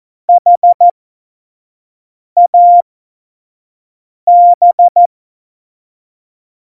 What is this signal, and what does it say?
Morse code, heard as a keyed tone, about 14 wpm, spaced as for 5 wpm, 700 Hz. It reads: HAB
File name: Morsekurs / Witze / Witz 02. Oktober 2018, 25 F70